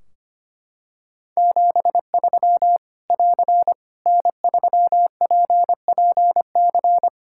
Morse code, heard as a keyed tone, about 25 wpm, 700 Hz. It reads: 73 <AR> N3PPC
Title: K1USN SST QSO Practice 25wpm